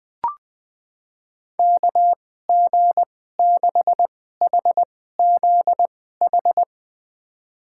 Morse code, heard as a keyed tone, about 20 wpm, spaced as for 14 wpm, 700 Hz. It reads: KG6HZH
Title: USA Call Signs - 6 Characters - 2x Character Spacing - 20wpm